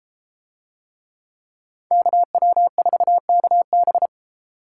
Morse code, heard as a keyed tone, about 32 wpm, 700 Hz. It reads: KW4KB